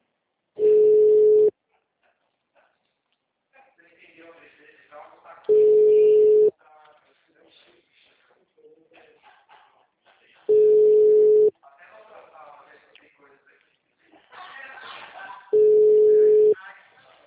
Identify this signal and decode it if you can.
Ringback tone (Europe, 425 Hz), with speech.